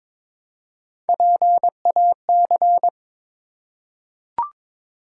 Morse code, heard as a keyed tone, about 22 wpm, 700 Hz. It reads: PAC
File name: ARRL Sections v2 - 22wpm